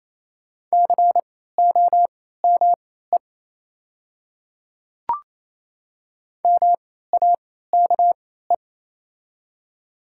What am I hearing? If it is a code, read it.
Morse code, heard as a keyed tone, about 27 wpm, spaced as for 16 wpm, 700 Hz. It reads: COME MAKE
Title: 3x Character Spacing - Top 100 Words - 28wpm